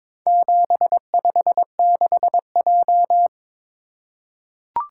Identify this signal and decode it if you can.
Morse code, heard as a keyed tone, about 22 wpm, 700 Hz. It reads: 756J
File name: practice / 22wpm / Four Letter-Number Combinations - 22wpm